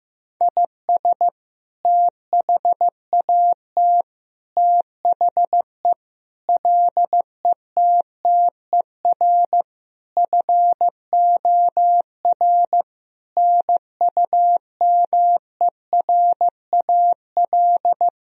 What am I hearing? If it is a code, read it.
Morse code, heard as a keyed tone, about 15 wpm, 700 Hz. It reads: IS THAT THE LETTER FOR NUMERAL
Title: Sentences from Top 1000 Words - Review Part 2 - 15wpm